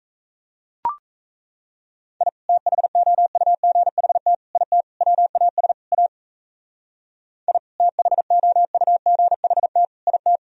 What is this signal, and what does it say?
Morse code, heard as a keyed tone, about 35 wpm, 700 Hz. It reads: I THOUGHT IT WAS A I THOUGHT IT
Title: Speed-Racing - Sets of 5 Words 28wpm